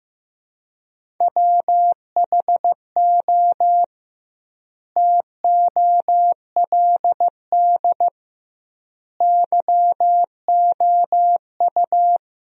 Morse code, heard as a keyed tone, about 16 wpm, 700 Hz. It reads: WHO TOLD YOU
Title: 2x Word-Spacing - Sentences from Top 400 Words - 15wpm